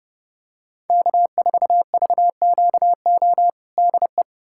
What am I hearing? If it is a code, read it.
Morse code, heard as a keyed tone, about 32 wpm, 700 Hz. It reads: K4VQO DE
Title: USA Callsign DE Callsign - 5 Characters - 30wpm